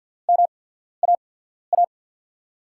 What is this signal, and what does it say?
Morse code, heard as a keyed tone, about 52 wpm, spaced as for 14 wpm, 700 Hz. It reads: MAA
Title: USA Call Signs - 5 Characters - 8x Character Spacing - 50wpm